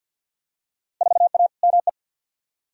Morse code, heard as a keyed tone, about 52 wpm, 700 Hz. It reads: 4A ME
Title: Field Day Exchanges - 50wpm